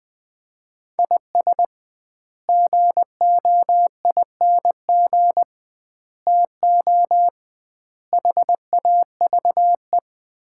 Morse code, heard as a keyed tone, about 20 wpm, 700 Hz. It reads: IS GOING TO HAVE
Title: 2x Word-Spacing - Sets of 5 Words - 20wpm